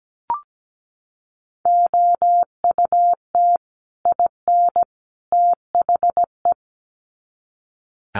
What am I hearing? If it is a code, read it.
Morse code, heard as a keyed tone, about 17 wpm, 700 Hz. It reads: OUT IN THE
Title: Sets of 3 Words 17wpm